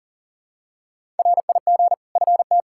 Morse code, heard as a keyed tone, about 40 wpm, 700 Hz. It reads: RIG FT
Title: K1USN SST QSO Practice 40wpm